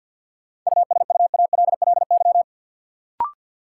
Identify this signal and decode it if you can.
Morse code, heard as a keyed tone, about 53 wpm, 700 Hz. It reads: USUALLY
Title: Sentences from Top 1000 Words - Review Part 3 - New Word Drill - 50wpm